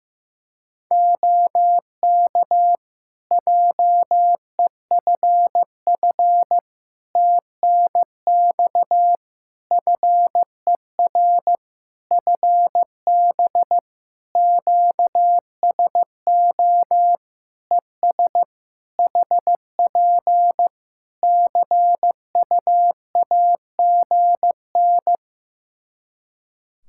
Morse code, heard as a keyed tone, about 15 wpm, 700 Hz. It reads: OK JEFF TNX FER FB QSO ES HP CUAGN
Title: Practice QSO - Exchange 4 - Ending - 15wpm